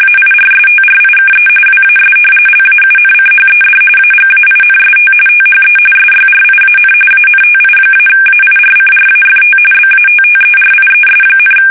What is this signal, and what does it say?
RTTY, about 45.45 baud. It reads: AIRSTRIKE WAS LAUNCHED AGAINST THE SYRIAN ANTI-AIRCRAFT BATTERIES,
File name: fsk850 / battleshipnj